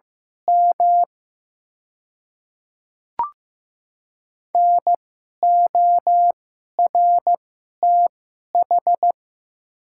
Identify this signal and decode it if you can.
Morse code, heard as a keyed tone, about 15 wpm, spaced as for 11 wpm, 700 Hz. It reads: M NORTH
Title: 2x Character Spacing - Top 500 Words - 15wpm